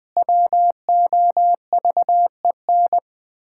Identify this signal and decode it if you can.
Morse code, heard as a keyed tone, about 20 wpm, 700 Hz. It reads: WOVEN
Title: Pangram Challenge v2 - 20wpm